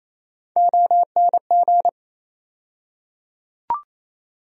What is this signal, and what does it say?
Morse code, heard as a keyed tone, about 28 wpm, 700 Hz. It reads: ONG
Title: USA Call Sign Postfixes - 3 Letters 28wpm